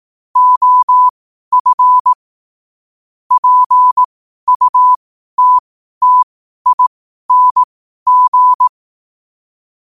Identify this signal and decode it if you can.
Morse code, heard as a keyed tone, about 18 wpm, spaced as for 12 wpm, 1000 Hz. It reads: OF PUTTING
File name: qotdfiles / 12wpm